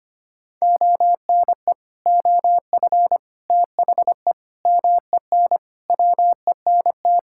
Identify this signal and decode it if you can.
Morse code, heard as a keyed tone, about 25 wpm, 700 Hz. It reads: ONE OF THE MEN WENT
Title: Sentences from Top 1600 Words 25wpm